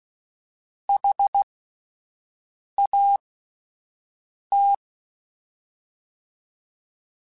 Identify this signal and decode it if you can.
Morse code, heard as a keyed tone, about 16 wpm, spaced as for 5 wpm, 800 Hz. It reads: HAT
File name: Alphabetical / Alphabetical mp3s / Hs @5